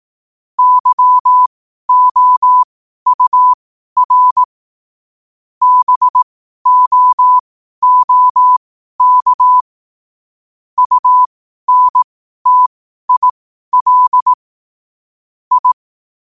Morse code, heard as a keyed tone, about 18 wpm, spaced as for 12 wpm, 1000 Hz. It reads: YOUR BOOK UNTIL I
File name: qotdfiles / 12wpm